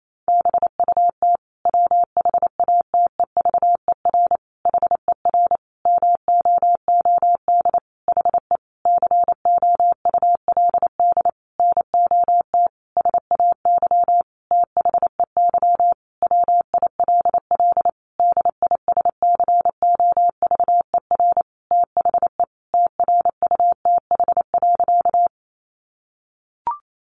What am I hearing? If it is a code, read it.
Morse code, heard as a keyed tone, about 28 wpm, 700 Hz. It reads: BUT WHATEVER HER MOOD HE COULD NOT SAY THEY WILL DISCOVER THE TRUTH.
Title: Sentences from Top 1100 Words 28wpm